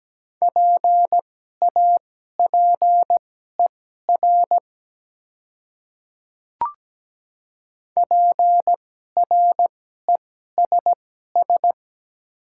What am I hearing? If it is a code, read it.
Morse code, heard as a keyed tone, about 17 wpm, spaced as for 12 wpm, 700 Hz. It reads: PAPER PRESS
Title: 2x Character Spacing - Top 300 Words - 17wpm